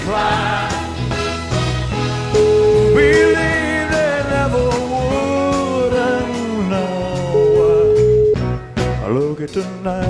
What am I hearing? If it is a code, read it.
Ringback tone (Europe, 425 Hz), with noise.